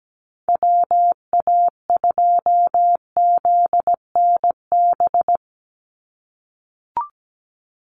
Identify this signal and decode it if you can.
Morse code, heard as a keyed tone, about 17 wpm, 700 Hz. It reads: WA2ZNB
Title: USA Call Signs 17wpm